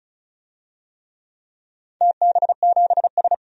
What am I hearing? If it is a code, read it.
Morse code, heard as a keyed tone, about 36 wpm, 700 Hz. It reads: TB7S
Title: DXCC-100 Call Signs - 4 Characters - 35wpm